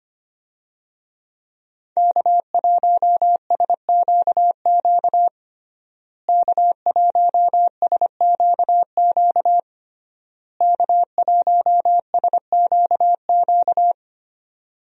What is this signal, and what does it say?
Morse code, heard as a keyed tone, about 25 wpm, 700 Hz. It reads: K1SQQ K1SQQ K1SQQ
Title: USA Call Signs - 5 Characters - Repeated 3x - 25wpm